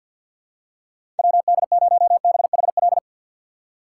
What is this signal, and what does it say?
Morse code, heard as a keyed tone, about 52 wpm, 700 Hz. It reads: WD0BHL
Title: USA Call Signs - 3x Character Spacing - 50wpm